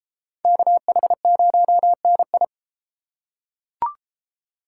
Morse code, heard as a keyed tone, about 32 wpm, 700 Hz. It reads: KH0NI